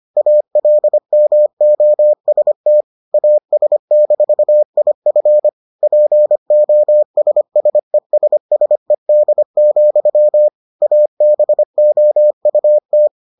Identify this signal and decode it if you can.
Morse code, heard as a keyed tone, about 25 wpm, 600 Hz. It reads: ALMOST AS-IF POSSESSED, ABOUT